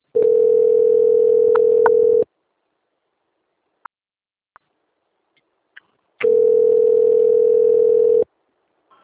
Ringback tone (North America), 440 + 480 Hz.
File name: out- - - - - .10918